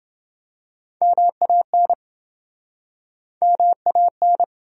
Morse code, heard as a keyed tone, about 30 wpm, 700 Hz. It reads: MAN MAN